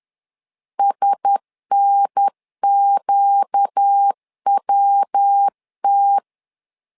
Morse code, heard as a keyed tone, about 10 wpm, 800 Hz. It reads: SNQWT
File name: LEZIONI / 4 Q1S 10wpm 2133